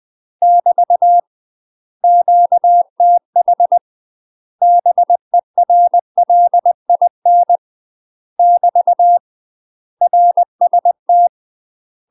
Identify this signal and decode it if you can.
Morse code, heard as a keyed tone, about 20 wpm, 700 Hz. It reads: <BT> QTH BERLIN <BT> RST
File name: UK MASTER QSOs TRAINING / 20WPM / UK MASTER QSOs 0020